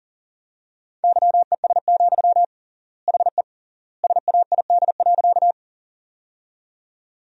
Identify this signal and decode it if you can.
Morse code, heard as a keyed tone, about 40 wpm, 700 Hz. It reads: YES, HE SAID.
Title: 3x Word-Spacing - Sentences from Top 500 Words - 40wpm